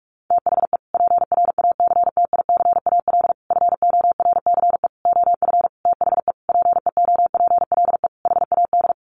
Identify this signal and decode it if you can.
Morse code, heard as a keyed tone, about 44 wpm, 700 Hz. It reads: THE PRACTICAL FORCE OF THE PEOPLE HAD
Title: Sentences from Top 1400 Words 45wpm